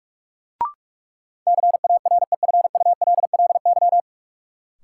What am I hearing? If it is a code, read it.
Morse code, heard as a keyed tone, about 46 wpm, 700 Hz. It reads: CAREFULLY
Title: Sentences from Top 1300 Words - New Word Drill Alt - 45wpm